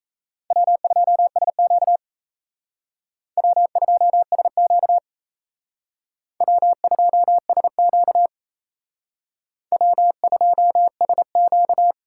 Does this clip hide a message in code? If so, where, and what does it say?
Yes: Morse code from 0.5 to 11.9 s, heard as a keyed tone, about 37 wpm, 700 Hz. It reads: W2SQ W2SQ W2SQ W2SQ